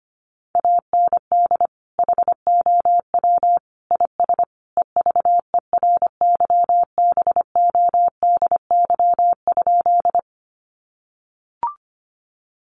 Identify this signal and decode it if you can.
Morse code, heard as a keyed tone, about 25 wpm, 700 Hz. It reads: AND HOW IS EVERYBODY?
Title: Sentences from Top 1400 Words 25wpm